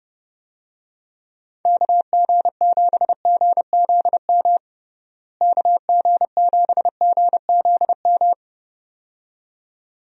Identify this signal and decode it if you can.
Morse code, heard as a keyed tone, about 30 wpm, 700 Hz. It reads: KG7GZM KG7GZM
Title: USA Call Signs - 6 Characters - Repeated 2x - 30wpm